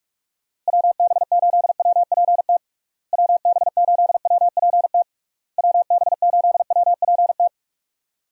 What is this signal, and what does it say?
Morse code, heard as a keyed tone, about 46 wpm, 700 Hz. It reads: WB8WPT WB8WPT WB8WPT
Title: USA Call Signs - 6 Characters - Repeated 3x - 45wpm